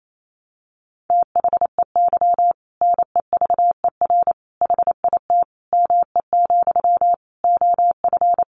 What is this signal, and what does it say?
Morse code, heard as a keyed tone, about 28 wpm, 700 Hz. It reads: THEY NEVER HIT ME, OF
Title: Sentences from Top 1200 Words 28wpm